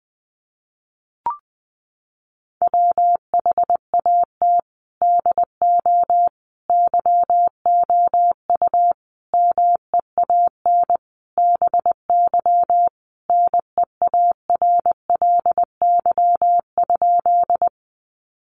Morse code, heard as a keyed tone, about 20 wpm, 700 Hz. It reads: WHAT DO YOU MEAN BY NEARLY?
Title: Sentences from Top 1300 Words 20wpm